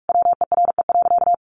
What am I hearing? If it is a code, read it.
Morse code, heard as a keyed tone, about 46 wpm, 700 Hz. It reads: WERE.
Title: Sentences from Top 1000 Words 45wpm